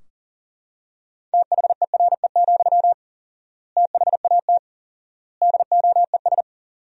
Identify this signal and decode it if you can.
Morse code, heard as a keyed tone, about 40 wpm, 700 Hz. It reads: THERE, THAT DOES
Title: 4x Word-Spacing - Sentences from Top 200 Words - 40wpm